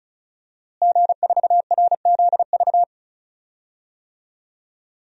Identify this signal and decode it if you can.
Morse code, heard as a keyed tone, about 35 wpm, 700 Hz. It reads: G4RZV
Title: DXCC-100 Call Signs - 5 Characters - Repeated 3x - 35wpm